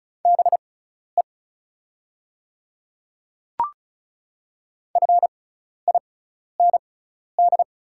Morse code, heard as a keyed tone, about 34 wpm, spaced as for 12 wpm, 700 Hz. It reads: BE FIND
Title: 6x Character Spacing - Top 200 Words - 35wpm